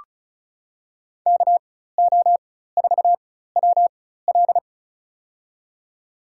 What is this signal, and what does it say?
Morse code, heard as a keyed tone, about 34 wpm, spaced as for 16 wpm, 700 Hz. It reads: KO4WL
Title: USA Call Signs - 5 Characters - 4x Character Spacing - 35wpm